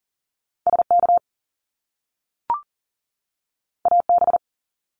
Morse code, heard as a keyed tone, about 39 wpm, 700 Hz. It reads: SK AB
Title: Canadian Provinces and Territories 40wpm